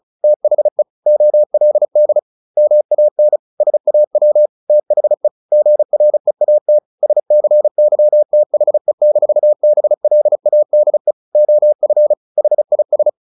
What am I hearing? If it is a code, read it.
Morse code, heard as a keyed tone, about 34 wpm, 600 Hz. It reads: THE OLD MAN SAW THE GREAT SCYTHE-BLADE OF HIS
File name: the old man and the sea / the old man and the sea 1 35wpm